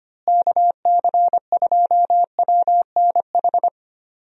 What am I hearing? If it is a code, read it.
Morse code, heard as a keyed tone, about 25 wpm, 700 Hz. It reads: KC2WNH
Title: USA Call Signs - 6 Characters - Repeated 3x - 25wpm